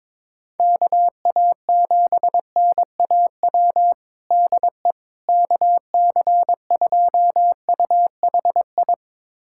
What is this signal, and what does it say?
Morse code, heard as a keyed tone, about 22 wpm, 700 Hz. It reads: KA7NAW DE KC2UHI
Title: USA Callsign DE Callsign 22wpm